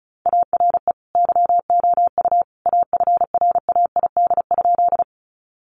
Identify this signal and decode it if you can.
Morse code, heard as a keyed tone, about 34 wpm, 700 Hz. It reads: ARE YOU AFRAID?
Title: Sentences from Top 1500 Words 35wpm